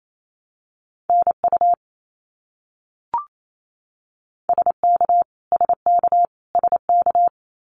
Morse code, heard as a keyed tone, about 28 wpm, 700 Hz. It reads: NU SK SK SK
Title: Canadian Provinces and Territories - Repeated 3x - 28wpm